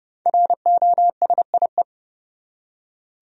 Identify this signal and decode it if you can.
Morse code, heard as a keyed tone, about 32 wpm, 700 Hz. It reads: ROSIE
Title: USA Common Names - 30wpm